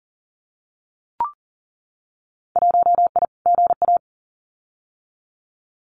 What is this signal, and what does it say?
Morse code, heard as a keyed tone, about 40 wpm, 700 Hz. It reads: 1I GA
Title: Winter Field Day Exchanges Encore 40wpm